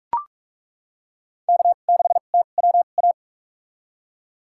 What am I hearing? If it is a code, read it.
Morse code, heard as a keyed tone, about 42 wpm, spaced as for 31 wpm, 700 Hz. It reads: K6TWA
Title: USA Call Signs - 5 Characters - 2x Character Spacing - 45wpm